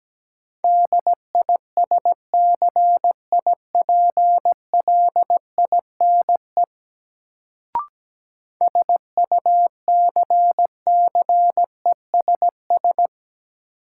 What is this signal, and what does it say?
Morse code, heard as a keyed tone, about 17 wpm, 700 Hz. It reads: DISCIPLINE SUCCESS
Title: Sentences from Top 900 Words - New Word Drill - 17wpm